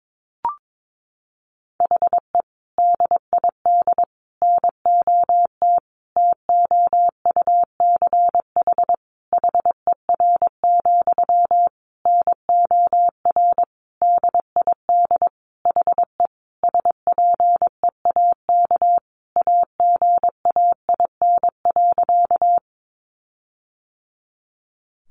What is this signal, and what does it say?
Morse code, heard as a keyed tone, about 22 wpm, 700 Hz. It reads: HE DID NOT TOUCH HER, NOR DID HE SPEAK AGAIN.
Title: Sentences from Top 1500 Words 22wpm